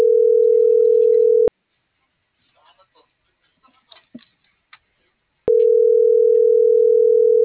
Ringback tone (North America), 440 + 480 Hz.